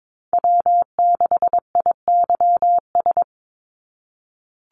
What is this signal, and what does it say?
Morse code, heard as a keyed tone, about 22 wpm, 700 Hz. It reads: W6IYS
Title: NA SOTA Chaser Callsigns 22wpm